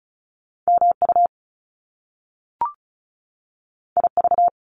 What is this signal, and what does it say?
Morse code, heard as a keyed tone, about 36 wpm, 700 Hz. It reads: MU IV
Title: Top Two Letter Combinations in Words 35wpm